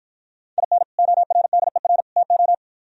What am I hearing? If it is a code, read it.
Morse code, heard as a keyed tone, about 51 wpm, 700 Hz. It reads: IN ORDER TO